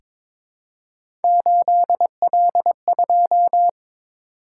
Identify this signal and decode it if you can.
Morse code, heard as a keyed tone, about 22 wpm, 700 Hz. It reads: 8L2
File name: practice / 22wpm / Three Letter-Number Combinations - 22wpm